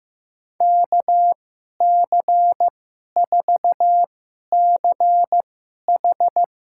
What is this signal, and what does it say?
Morse code, heard as a keyed tone, about 15 wpm, spaced as for 11 wpm, 700 Hz. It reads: KC4CH
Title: USA Call Signs - 2x Character Spacing - 15wpm